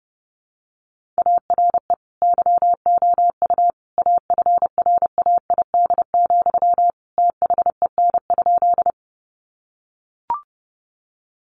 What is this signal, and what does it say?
Morse code, heard as a keyed tone, about 28 wpm, 700 Hz. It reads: ARE YOU AFRAID, THEN?
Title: Sentences from Top 1500 Words 30wpm